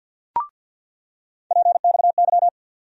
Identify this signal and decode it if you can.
Morse code, heard as a keyed tone, about 47 wpm, 700 Hz. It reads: PXY
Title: USA Call Sign Postfixes - 3 Letters 50wpm